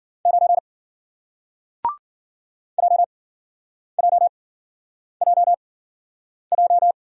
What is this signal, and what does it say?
Morse code, heard as a keyed tone, about 53 wpm, spaced as for 9 wpm, 700 Hz. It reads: 9 JJJJ